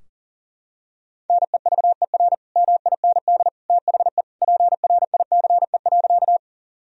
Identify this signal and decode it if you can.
Morse code, heard as a keyed tone, about 40 wpm, 700 Hz. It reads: NEVER MIND THE PRICE.
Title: Sentences from Top 1000 Words - Review Part 4 - 40wpm